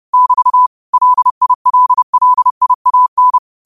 Morse code, heard as a keyed tone, about 30 wpm, 1000 Hz. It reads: <BT> LILLIAN
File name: qotdfiles / 30wpm